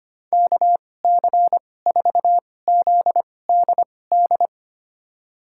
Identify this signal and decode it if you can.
Morse code, heard as a keyed tone, about 25 wpm, spaced as for 18 wpm, 700 Hz. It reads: KC4ZDD
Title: USA Call Signs - 2x Character Spacing - 25wpm